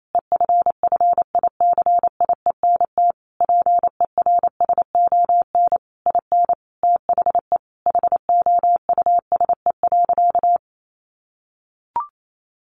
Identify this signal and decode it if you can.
Morse code, heard as a keyed tone, about 28 wpm, 700 Hz. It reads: EFFICIENT PERSON IN THE HOUSE.
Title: Sentences from Top 1300 Words 28wpm